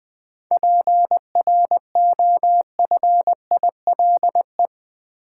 Morse code, heard as a keyed tone, about 20 wpm, 700 Hz. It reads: PROFILE
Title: Seven Letter Words - 20wpm